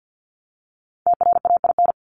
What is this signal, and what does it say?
Morse code, heard as a keyed tone, about 50 wpm, 700 Hz. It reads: TRAIN
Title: Top 1751-2000 Words 50wpm